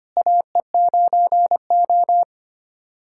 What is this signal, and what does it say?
Morse code, heard as a keyed tone, about 25 wpm, 700 Hz. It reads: AE9O
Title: USA Callsign DE Callsign - 4 Characters - 25wpm